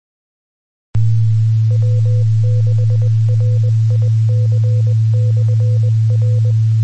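Morse code, heard as a keyed tone, about 20 wpm, 500 Hz, with noise. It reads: W6RIC/R
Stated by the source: Morse code recorded on the 2 m band at 146.850 MHz.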